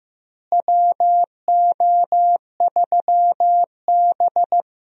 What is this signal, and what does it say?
Morse code, heard as a keyed tone, about 16 wpm, 700 Hz. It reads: WO3B
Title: USA Call Signs - 4 Characters - Repeated 3x - 15wpm